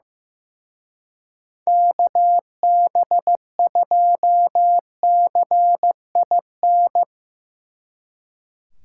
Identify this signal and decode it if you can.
Morse code, heard as a keyed tone, about 16 wpm, 700 Hz. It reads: KB2CIN
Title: Speed Racing - USA 2x3 Call Signs 15wpm